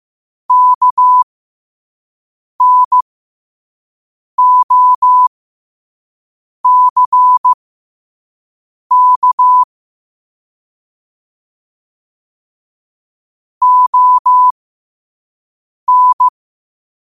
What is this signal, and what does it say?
Morse code, heard as a keyed tone, about 15 wpm, spaced as for 5 wpm, 1000 Hz. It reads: KNOCK ON
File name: qotdfiles / 5wpm